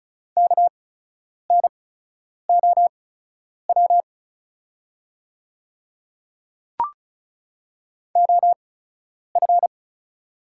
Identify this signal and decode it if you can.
Morse code, heard as a keyed tone, about 36 wpm, spaced as for 10 wpm, 700 Hz. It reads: KNOW OF
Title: 8x Character Spacing - Top 500 Words - 35wpm